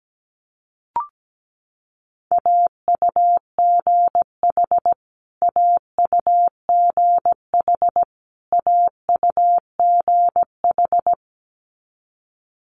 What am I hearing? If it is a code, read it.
Morse code, heard as a keyed tone, about 17 wpm, 700 Hz. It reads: AUGH AUGH AUGH
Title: Multi-letter Phonemes Repeated 3x 17wpm